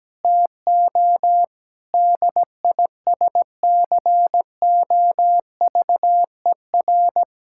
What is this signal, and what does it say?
Morse code, heard as a keyed tone, about 17 wpm, 700 Hz. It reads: TO DISCOVER
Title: Sentences from Top 1600 Words 17wpm